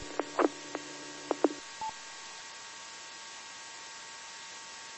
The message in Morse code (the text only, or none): none